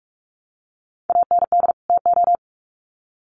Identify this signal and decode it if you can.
Morse code, heard as a keyed tone, about 44 wpm, 700 Hz. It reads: AND TO
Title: Sets of 2 Words 45wpm